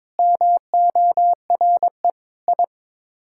Morse code, heard as a keyed tone, about 22 wpm, 700 Hz. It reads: MORE I
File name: Sets of 9 Words - 22wpm